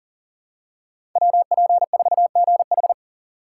Morse code, heard as a keyed tone, about 40 wpm, 700 Hz. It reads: WP4GH
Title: USA Call Signs - 4x Character Spacing - 40wpm